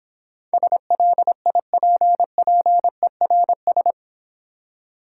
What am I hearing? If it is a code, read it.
Morse code, heard as a keyed tone, about 26 wpm, 700 Hz. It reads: SLIPPERS